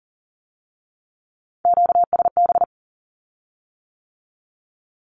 Morse code, heard as a keyed tone, about 38 wpm, 700 Hz. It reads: QSB
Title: Q-Codes with Spoken Meaning - 40wpm